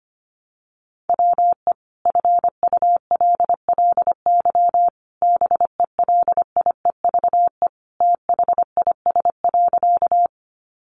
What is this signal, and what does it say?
Morse code, heard as a keyed tone, about 25 wpm, 700 Hz. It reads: WE FULLY BELIEVE THIS.
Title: Sentences from Top 1300 Words 25wpm